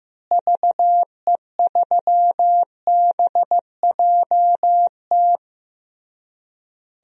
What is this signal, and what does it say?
Morse code, heard as a keyed tone, about 15 wpm, 700 Hz. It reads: VE3BJT
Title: DXCC-100 Call Signs - 6 Characters - Repeated 3x - 15wpm